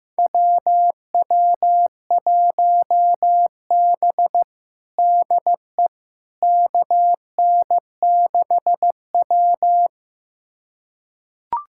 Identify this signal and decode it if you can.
Morse code, heard as a keyed tone, about 15 wpm, 700 Hz. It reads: WW1B DE KN6W
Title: USA Callsign DE Callsign - 4 Characters - 15wpm